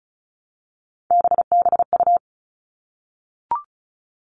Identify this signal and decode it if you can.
Morse code, heard as a keyed tone, about 34 wpm, 700 Hz. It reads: BBU
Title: All Cut Numbers - 3-Digit 35wpm